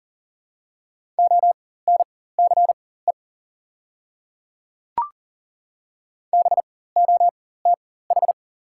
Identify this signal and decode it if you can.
Morse code, heard as a keyed tone, about 40 wpm, spaced as for 19 wpm, 700 Hz. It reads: ONCE BOTH
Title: 4x Character Spacing - Top 300 Words - 40wpm